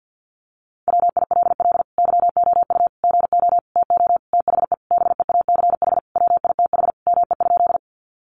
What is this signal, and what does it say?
Morse code, heard as a keyed tone, about 49 wpm, 700 Hz. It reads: WILL YOU GO TO THE BEACH WITH ME?
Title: Sentences from Top 1500 Words 50wpm